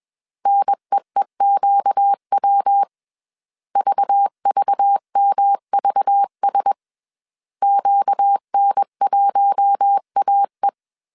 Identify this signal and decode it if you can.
Morse code, heard as a keyed tone, about 21 wpm, 800 Hz. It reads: NEEQW VVMVS QN1AE
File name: CW / DET / 1 6 1500C S B 20wpm 2048